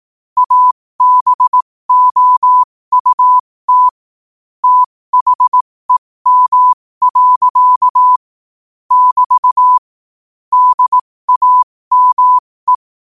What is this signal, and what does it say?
Morse code, heard as a keyed tone, about 18 wpm, spaced as for 15 wpm, 1000 Hz. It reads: ABOUT THEM. <BT> DAME
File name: qotdfiles / 15wpm